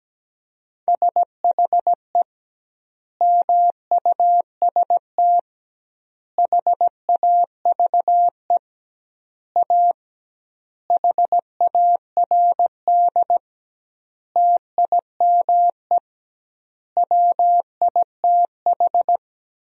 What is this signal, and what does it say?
Morse code, heard as a keyed tone, about 17 wpm, 700 Hz. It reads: SHE MUST HAVE A HARD TIME WITH
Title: 2x Word-Spacing - Sentences from Top 300 Words - 17wpm